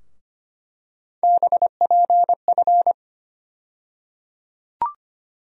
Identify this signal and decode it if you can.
Morse code, heard as a keyed tone, about 25 wpm, 700 Hz. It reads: BPF
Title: USA Call Sign Postfixes - 3 Letters 25wpm